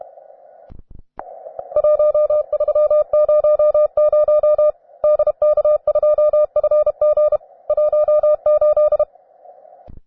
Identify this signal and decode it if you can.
Morse code, heard as a keyed tone, about 32 wpm, 600 Hz. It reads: 1300 DK2FG 18
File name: cw advanced